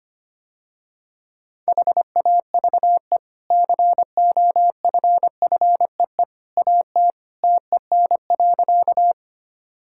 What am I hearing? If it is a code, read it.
Morse code, heard as a keyed tone, about 25 wpm, 700 Hz. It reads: HAVE COFFEE AT TEN.
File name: Easy Sentences with Hints - 25wpm